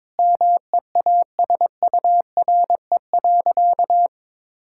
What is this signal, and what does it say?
Morse code, heard as a keyed tone, about 21 wpm, 700 Hz. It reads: MEASURE.
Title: Sentences from Top 1000 Words - Review Part 6 - 22wpm